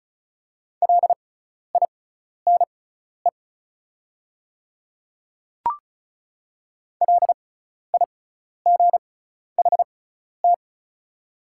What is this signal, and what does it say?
Morse code, heard as a keyed tone, about 34 wpm, spaced as for 12 wpm, 700 Hz. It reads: LINE LIGHT
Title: 6x Character Spacing - Top 500 Words - 35wpm